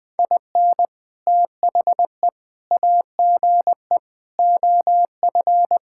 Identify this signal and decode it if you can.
Morse code, heard as a keyed tone, about 21 wpm, 700 Hz. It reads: IN THE AGE OF
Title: Sentences with 3-Word Phrases - Spoken Beginning - 20wpm